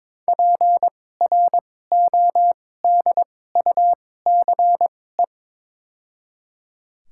Morse code, heard as a keyed tone, about 21 wpm, spaced as for 16 wpm, 700 Hz. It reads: PRODUCE
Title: 2x Character Spacing - Top 500 Words - 22wpm